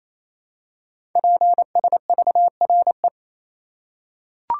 Morse code, heard as a keyed tone, about 27 wpm, 700 Hz. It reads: PSVRE
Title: Code Groups - 5 Letter:Numbers - 28wpm